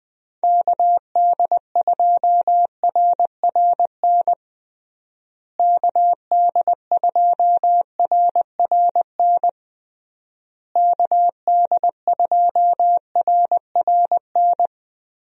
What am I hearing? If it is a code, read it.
Morse code, heard as a keyed tone, about 20 wpm, 700 Hz. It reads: KD2RRN KD2RRN KD2RRN
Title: USA Call Signs - 6 Characters - Repeated 3x - 20wpm